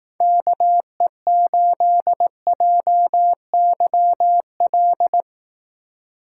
Morse code, heard as a keyed tone, about 18 wpm, 700 Hz. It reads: KE8JYL
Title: Speed Racing - USA Call Signs 15wpm